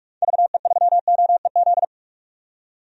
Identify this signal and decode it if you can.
Morse code, heard as a keyed tone, about 46 wpm, 700 Hz. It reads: VE3OEZ
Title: DXCC-100 Call Signs - 6 Characters - 45wpm